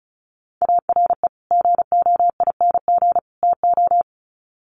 Morse code, heard as a keyed tone, about 34 wpm, 700 Hz. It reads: ARE GOING TO
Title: Sets of 3 Words 35wpm